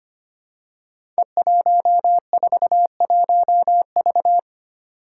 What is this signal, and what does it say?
Morse code, heard as a keyed tone, about 25 wpm, 700 Hz. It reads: E141V
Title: Five Challenging Letter-Number Combinations 25wpm